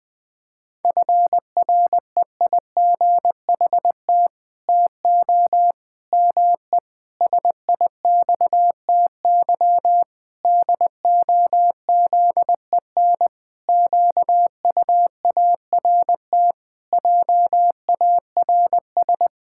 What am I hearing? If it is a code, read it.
Morse code, heard as a keyed tone, about 20 wpm, 700 Hz. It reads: FREIGHT TO ME SIXTY DOZEN QUART JARS
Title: Pangram Challenge v2 - 20wpm